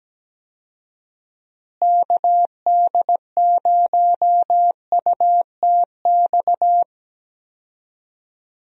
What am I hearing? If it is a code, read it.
Morse code, heard as a keyed tone, about 17 wpm, 700 Hz. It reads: KD0UTX
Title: USA Call Signs - 6 Characters - Repeated 2x - 17wpm